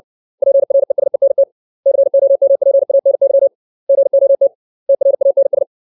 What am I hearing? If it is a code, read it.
Morse code, heard as a keyed tone, about 60 wpm, 550 Hz. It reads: PRESENT COMPANY GOT TRANS
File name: 632 words / 632 Words 60WPM